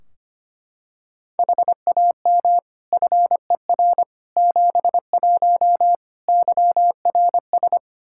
Morse code, heard as a keyed tone, about 25 wpm, 700 Hz. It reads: HAM FER 71 YRS
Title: Practice QSO - Exchange 3 - More Info - 25wpm